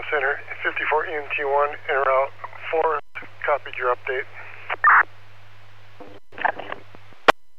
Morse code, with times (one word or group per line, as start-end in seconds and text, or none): none